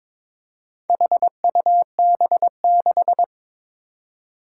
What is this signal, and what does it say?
Morse code, heard as a keyed tone, about 22 wpm, 700 Hz. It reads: HUB6
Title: Four Challenging Letter-Number Combinations 22wpm